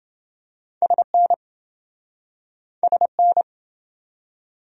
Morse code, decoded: SN SN

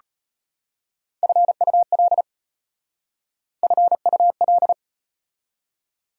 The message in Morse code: FUL FUL

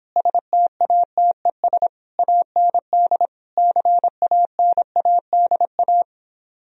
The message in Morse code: STATES AND CANADA

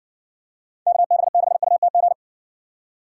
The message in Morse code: KB6UTG

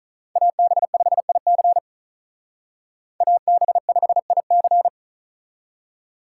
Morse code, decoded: AB5IC AB5IC

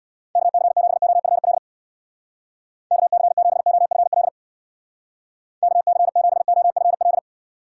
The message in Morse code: KC7OFL KC7OFL KC7OFL

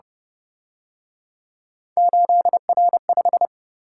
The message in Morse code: 8R5

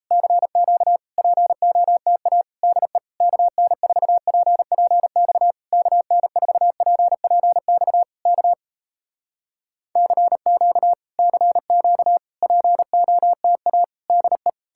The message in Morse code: CQ POTA DE KN4PPX KN4PPX K CQ CQ POTA DE

